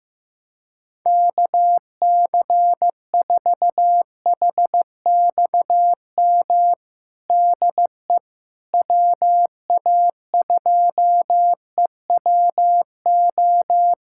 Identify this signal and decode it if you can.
Morse code, heard as a keyed tone, about 15 wpm, 700 Hz. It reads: KC4HXM DE WA2EWO
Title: USA Callsign DE Callsign 15wpm